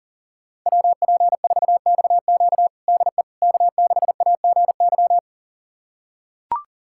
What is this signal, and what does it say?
Morse code, heard as a keyed tone, about 40 wpm, 700 Hz. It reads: WP4XQ DE K6AGY
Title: USA Callsign DE Callsign 40wpm